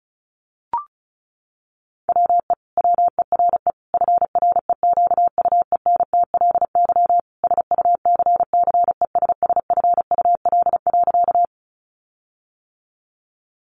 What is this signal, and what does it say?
Morse code, heard as a keyed tone, about 34 wpm, 700 Hz. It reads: WE WERE FREQUENTLY SUCCESSFUL.
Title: Sentences from Top 1300 Words 35wpm